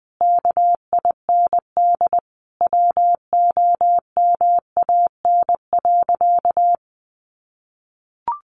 Morse code, heard as a keyed tone, about 20 wpm, 700 Hz. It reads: KIND WOMAN.
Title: Sentences from Top 1100 Words 20wpm